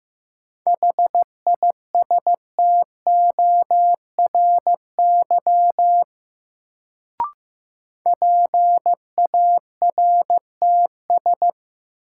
Morse code, heard as a keyed tone, about 15 wpm, 700 Hz. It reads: HISTORY PARTS
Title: Sets of 3 Words - New Word Drill - 15wpm